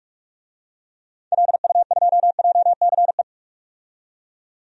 Morse code, heard as a keyed tone, about 41 wpm, 700 Hz. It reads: LU2JCE
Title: DXCC-100 Call Signs - 6 Characters - 45wpm